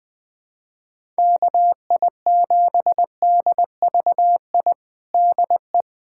Morse code, heard as a keyed tone, about 20 wpm, 700 Hz. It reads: KI7DVI DE
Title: USA Callsign DE Callsign - 6 Characters - 20wpm